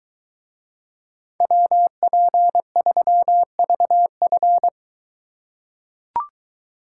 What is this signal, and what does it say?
Morse code, heard as a keyed tone, about 22 wpm, 700 Hz. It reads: WP3VF